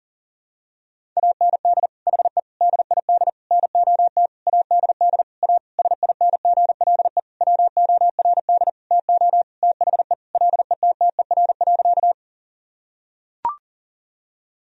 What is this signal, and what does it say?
Morse code, heard as a keyed tone, about 40 wpm, 700 Hz. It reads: AND HE DID NOT ADD A SINGLE WORD TO THE LETTER.
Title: Sentences from Top 1000 Words - Review Part 1 - 40wpm